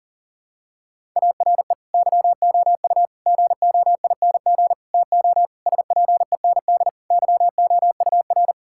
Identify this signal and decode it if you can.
Morse code, heard as a keyed tone, about 40 wpm, 700 Hz. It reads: ARE YOU GOING TO SPEND YOUR
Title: Sentences from Top 1000 Words - Review Part 5 - 40wpm